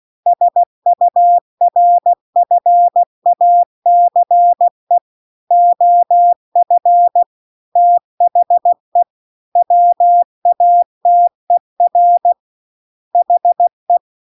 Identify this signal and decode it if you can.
Morse code, heard as a keyed tone, about 16 wpm, 700 Hz. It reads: SURFACE OF THE WATER HE
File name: CW-The-Big-Twoheaded-River-2-16WPM